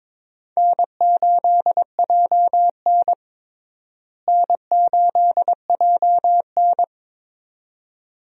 Morse code, heard as a keyed tone, about 22 wpm, 700 Hz. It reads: N8JN N8JN
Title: USA Call Signs - 4 Characters - Repeated 2x - 22wpm